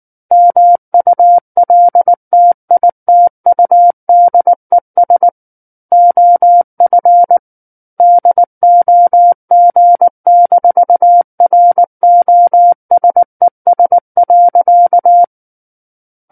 Morse code, heard as a keyed tone, about 19 wpm, 700 Hz. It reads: MULTITUDES OF DOG-ROSES.
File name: Book One / 10-Chapter 10 - 19 WPM